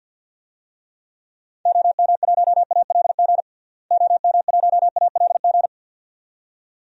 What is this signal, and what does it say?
Morse code, heard as a keyed tone, about 50 wpm, 700 Hz. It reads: OM1ALG OM1ALG
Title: DXCC-100 Call Signs - 6 Characters - Repeated 2x - 50wpm